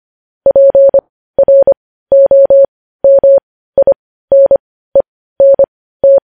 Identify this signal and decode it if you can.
Morse code, heard as a keyed tone, about 25 wpm, spaced as for 15 wpm, 550 Hz. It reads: PROMINENT
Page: news-headlines-in-morse-code-at-15-wpm